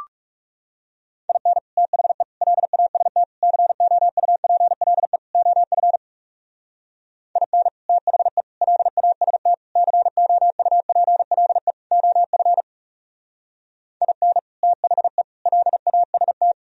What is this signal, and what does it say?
Morse code, heard as a keyed tone, about 38 wpm, 700 Hz. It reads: IN THE LAST COUPLE OF IN THE LAST COUPLE OF IN THE LAST
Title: Speed-Racing - Sets of 5 Words 30wpm